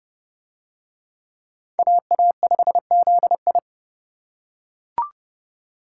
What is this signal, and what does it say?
Morse code, heard as a keyed tone, about 31 wpm, 700 Hz. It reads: AA5ZI